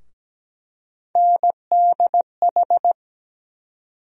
Morse code, heard as a keyed tone, about 17 wpm, 700 Hz. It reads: NDH
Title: USA Call Sign Postfixes - 3 Letters 17wpm